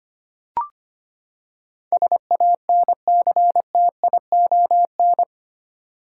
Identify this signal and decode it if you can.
Morse code, heard as a keyed tone, about 25 wpm, 700 Hz. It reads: SANCTION
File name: Eight Letter Words - 25wpm